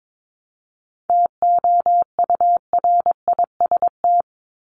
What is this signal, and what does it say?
Morse code, heard as a keyed tone, about 22 wpm, 700 Hz. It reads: TOURIST
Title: Top 1501-1750 Words 22wpm